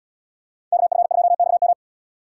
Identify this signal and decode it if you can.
Morse code, heard as a keyed tone, about 75 wpm, 700 Hz. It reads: KF2XM